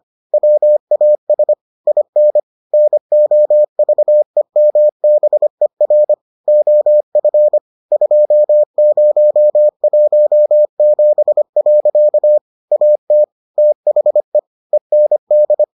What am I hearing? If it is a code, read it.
Morse code, heard as a keyed tone, about 25 wpm, 600 Hz. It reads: WAS IN NOVEMBER OF 2017. AT THE END